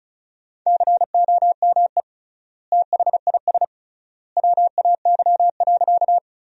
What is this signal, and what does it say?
Morse code, heard as a keyed tone, about 36 wpm, 700 Hz. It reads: COME THIS WAY.
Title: 3x Word-Spacing - Sentences from Top 100 Words - 35wpm